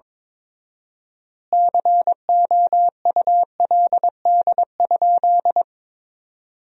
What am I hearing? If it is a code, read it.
Morse code, heard as a keyed tone, about 22 wpm, 700 Hz. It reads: COULD?
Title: 4x Word-Spacing - Sentences from Top 400 Words - 22wpm